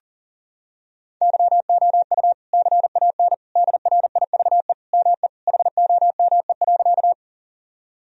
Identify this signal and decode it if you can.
Morse code, heard as a keyed tone, about 37 wpm, 700 Hz. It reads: YOU CAN DRIVE ME HOME.
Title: Sentences from Top 1000 Words - Review Part 2 - 40wpm